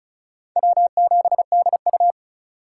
Morse code, heard as a keyed tone, about 45 wpm, 700 Hz. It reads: W7DU